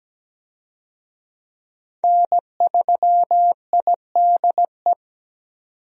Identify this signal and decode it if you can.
Morse code, heard as a keyed tone, about 17 wpm, 700 Hz. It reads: N3IDE
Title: USA Call Signs - 5 Characters - Repeated 2x - 17wpm